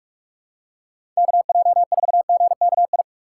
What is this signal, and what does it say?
Morse code, heard as a keyed tone, about 43 wpm, 700 Hz. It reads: KJ4GKI
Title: USA Call Signs - 6 Characters - 4x Character Spacing - 45wpm